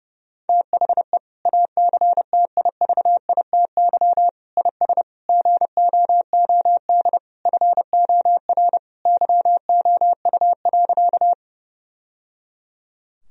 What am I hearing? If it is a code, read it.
Morse code, heard as a keyed tone, about 30 wpm, 700 Hz. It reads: THE ACTIVITY IS GOOD FOR YOU.
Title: Sentences from Top 1000 Words - Review Part 4 - 30wpm